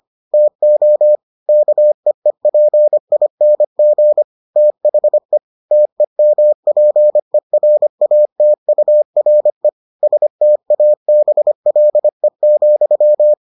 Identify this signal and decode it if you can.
Morse code, heard as a keyed tone, about 25 wpm, 600 Hz. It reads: TO KEEPING THE TEMPERATURE STABLE,